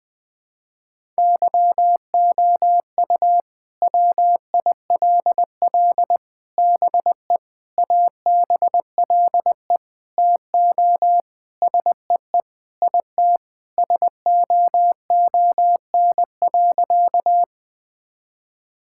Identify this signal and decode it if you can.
Morse code, heard as a keyed tone, about 21 wpm, 700 Hz. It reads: YOU WILL BE ABLE TO SEE IT SOON.
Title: Sentences with 4-Word Phrases - 20wpm